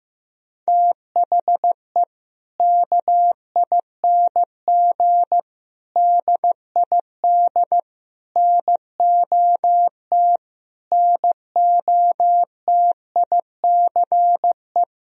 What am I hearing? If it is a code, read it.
Morse code, heard as a keyed tone, about 15 wpm, 700 Hz. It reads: THE KING DID NOT NOTICE
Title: Sentences from Top 1000 Words - Review Part 2 - 15wpm